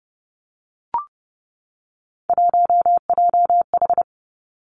Morse code, heard as a keyed tone, about 30 wpm, 700 Hz. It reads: 1JH
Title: Three Challenging Letter-Number Combinations 30wpm